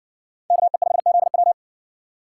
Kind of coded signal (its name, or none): Morse code